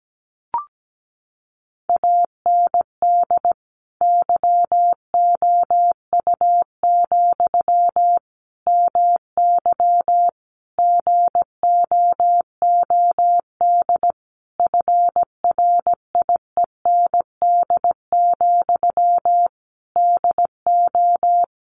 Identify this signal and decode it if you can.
Morse code, heard as a keyed tone, about 17 wpm, 700 Hz. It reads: AND YOU, MY GOOD FRIEND, DO
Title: Sentences from Top 1100 Words 17wpm